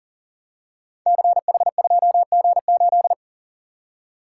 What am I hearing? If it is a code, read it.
Morse code, heard as a keyed tone, about 42 wpm, 700 Hz. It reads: CH2G8
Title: Code Groups - 5 Letter:Numbers - 40wpm